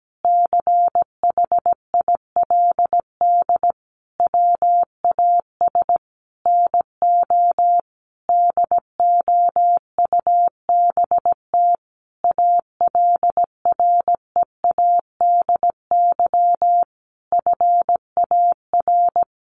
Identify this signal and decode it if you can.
Morse code, heard as a keyed tone, about 17 wpm, 700 Hz. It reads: CHILD WAS NO DOUBT ALREADY FAR
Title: Sentences from Top 1500 Words 17wpm